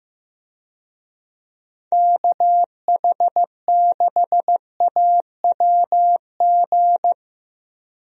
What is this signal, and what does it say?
Morse code, heard as a keyed tone, about 15 wpm, 700 Hz. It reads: KH6AWG
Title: DXCC-100 Call Signs - 6 Characters - Repeated 2x - 15wpm